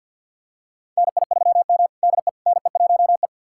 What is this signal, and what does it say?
Morse code, heard as a keyed tone, about 46 wpm, 700 Hz. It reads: NI3M DE NE1E
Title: USA Callsign DE Callsign - 4 Characters - 50wpm